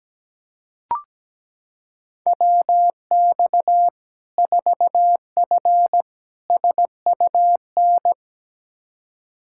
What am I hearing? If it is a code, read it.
Morse code, heard as a keyed tone, about 17 wpm, 700 Hz. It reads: WX 4F SUN
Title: Practice QSO - Exchange 2 - Info - 17wpm